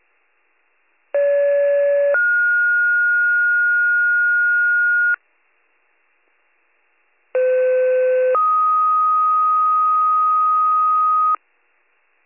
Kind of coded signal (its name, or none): two-tone page